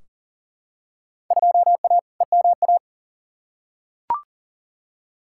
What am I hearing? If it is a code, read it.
Morse code, heard as a keyed tone, about 40 wpm, 700 Hz. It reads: 2A EMA
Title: Field Day Exchanges Encore - 40wpm